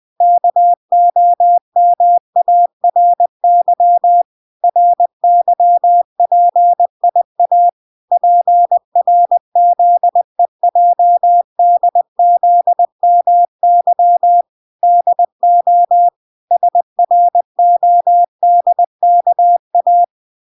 Morse code, heard as a keyed tone, about 20 wpm, 700 Hz. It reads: KOMARY RYPIA PRZEJDZMY DO SRODKA